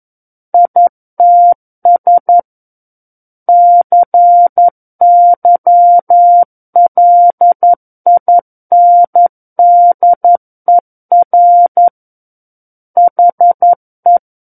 Morse code, heard as a keyed tone, about 11 wpm, 700 Hz. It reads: ITS CYLINDER HE
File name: Book One / 02-Chapter 2 - 11 WPM